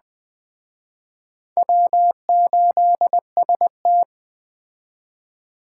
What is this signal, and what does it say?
Morse code, heard as a keyed tone, about 21 wpm, 700 Hz. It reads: W8ST
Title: Speed Racing - USA Call Signs 17wpm